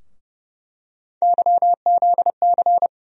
Morse code, heard as a keyed tone, about 30 wpm, 700 Hz. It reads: YZC
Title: USA Call Sign Postfixes - 3 Letters 30wpm